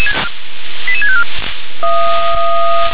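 Five-tone selective call (ZVEI), heard as quick runs of five tones, tones of about 70 ms. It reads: 88654
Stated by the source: Five-tone ZVEI selective call received on SDR.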